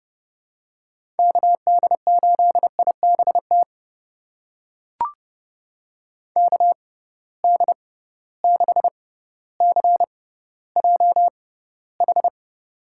Morse code, heard as a keyed tone, about 30 wpm, spaced as for 15 wpm, 700 Hz. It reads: KD8IBT KD6CJH
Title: USA Call Signs - 6 Characters - 6x Character Spacing - 30wpm